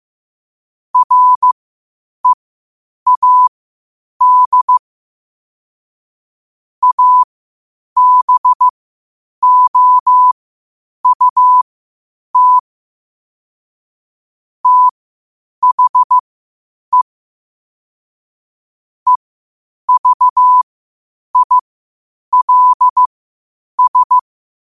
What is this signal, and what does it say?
Morse code, heard as a keyed tone, about 15 wpm, spaced as for 8 wpm, 1000 Hz. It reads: READ ABOUT THE EVILS
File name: qotdfiles / 8wpm